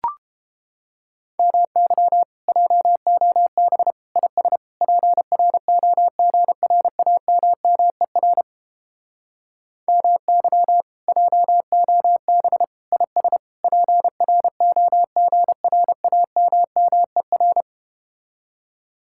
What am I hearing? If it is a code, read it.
Morse code, heard as a keyed tone, about 30 wpm, 700 Hz. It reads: MY JOB IS PROGRAMMER MY JOB IS PROGRAMMER